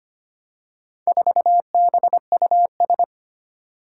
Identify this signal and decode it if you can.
Morse code, heard as a keyed tone, about 25 wpm, 700 Hz. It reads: 4BUS
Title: Four Challenging Letter-Number Combinations 25wpm